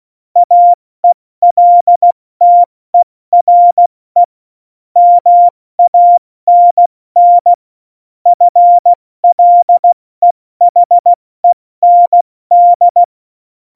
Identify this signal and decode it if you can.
Morse code, heard as a keyed tone, about 16 wpm, 700 Hz. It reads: AELTERE MANN FLEHEND
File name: Morsekurs / Witze / Witz 18. Juni 2019, 70 F80